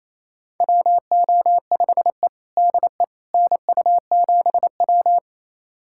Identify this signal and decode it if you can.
Morse code, heard as a keyed tone, about 28 wpm, 700 Hz. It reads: WO5E DE NU7W
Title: USA Callsign DE Callsign - 4 Characters - 28wpm